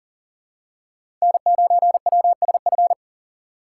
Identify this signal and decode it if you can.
Morse code, heard as a keyed tone, about 40 wpm, 700 Hz. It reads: N9WSF
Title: USA Call Signs - 6x Character Spacing - 40wpm